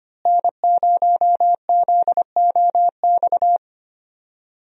Morse code, heard as a keyed tone, about 25 wpm, 700 Hz. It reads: N0ZOX
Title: USA Call Signs - 5 Characters - Repeated 2x - 25wpm